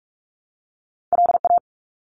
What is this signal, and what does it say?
Morse code, heard as a keyed tone, about 44 wpm, 700 Hz. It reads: LA
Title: ARRL Sections 45wpm